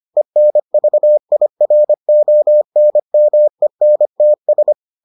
Morse code, heard as a keyed tone, about 25 wpm, 600 Hz. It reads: ENVIRONMENTS